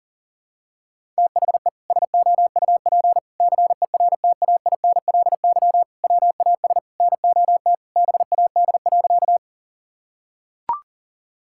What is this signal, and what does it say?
Morse code, heard as a keyed tone, about 37 wpm, 700 Hz. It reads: THE SOUP CERTAINLY WAS NOT BAD.
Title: Sentences from Top 1000 Words - Review Part 6 - 40wpm